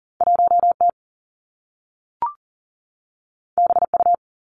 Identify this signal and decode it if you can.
Morse code, heard as a keyed tone, about 39 wpm, 700 Hz. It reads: 1T BU
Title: Two Challenging Letter-Number Combinations 40wpm